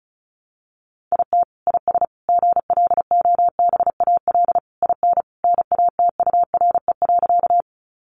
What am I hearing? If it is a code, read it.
Morse code, heard as a keyed tone, about 34 wpm, 700 Hz. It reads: IT IS GLOBAL IN NATURE.
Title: Sentences from Top 1200 Words 35wpm